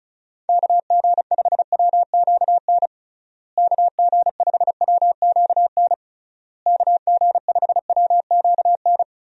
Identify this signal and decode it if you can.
Morse code, heard as a keyed tone, about 36 wpm, 700 Hz. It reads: KG5WQN KG5WQN KG5WQN
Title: USA Call Signs - 6 Characters - Repeated 3x - 35wpm